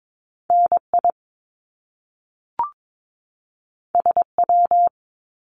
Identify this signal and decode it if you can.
Morse code, heard as a keyed tone, about 22 wpm, 700 Hz. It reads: NI SW
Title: Top Two Letter Combinations in Words 22wpm